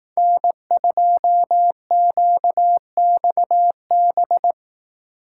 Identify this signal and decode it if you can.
Morse code, heard as a keyed tone, about 18 wpm, 700 Hz. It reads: N2QXB
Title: Speed Racing - USA 1x3 Call Signs 15wpm